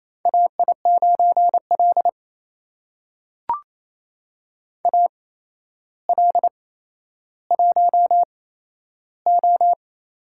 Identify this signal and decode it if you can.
Morse code, heard as a keyed tone, about 27 wpm, spaced as for 12 wpm, 700 Hz. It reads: AI9L AL1O
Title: USA Call Signs - 4 Characters - 8x Character Spacing - 28wpm